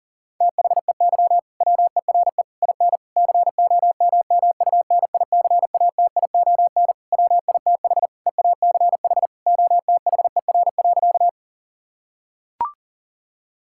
Morse code, heard as a keyed tone, about 37 wpm, 700 Hz. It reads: THEY WERE IN COMMUNICATION WITH EACH OTHER.
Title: Sentences from Top 1000 Words - Review Part 6 - 40wpm